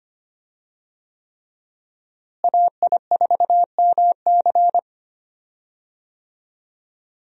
Morse code, heard as a keyed tone, about 25 wpm, 700 Hz. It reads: AI4MC
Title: USA Call Signs - 5 Characters - Repeated 3x - 25wpm